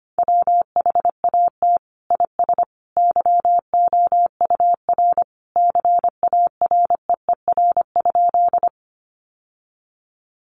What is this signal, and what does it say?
Morse code, heard as a keyed tone, about 25 wpm, 700 Hz. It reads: WHAT IS YOUR CAREER?
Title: Sentences from Top 1000 Words 25wpm